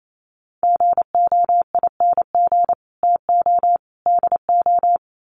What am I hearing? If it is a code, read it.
Morse code, heard as a keyed tone, about 28 wpm, 700 Hz. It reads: GOING TO DO
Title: Sets of 3 Words 28wpm